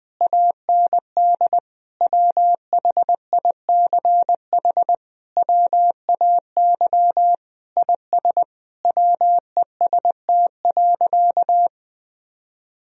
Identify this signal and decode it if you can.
Morse code, heard as a keyed tone, about 21 wpm, 700 Hz. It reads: AND WHICH WAY IS WEST.
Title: Sentences from Top 1000 Words - Review Part 3 - 20wpm